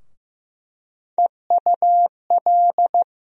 Morse code, heard as a keyed tone, about 15 wpm, 700 Hz. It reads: EUL
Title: USA Call Sign Postfixes - 3 Letters 15wpm